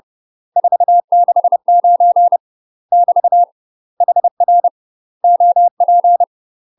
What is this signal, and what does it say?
Morse code, heard as a keyed tone, about 30 wpm, 700 Hz. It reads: 469 <BT> HR OP
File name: UK MASTER QSOs TRAINING / 30WPM / UK MASTER QSOs 0021